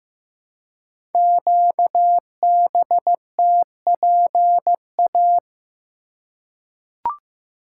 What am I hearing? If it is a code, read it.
Morse code, heard as a keyed tone, about 15 wpm, 700 Hz. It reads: QBTPA
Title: Code Groups - 5 Letters - 15wpm